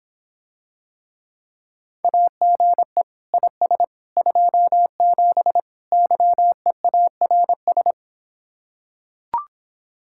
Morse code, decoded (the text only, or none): AGE IS 27 YEARS